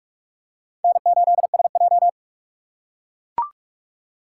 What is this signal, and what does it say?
Morse code, heard as a keyed tone, about 43 wpm, 700 Hz. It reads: N8SJ
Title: USA Call Signs - 3x Character Spacing - 45wpm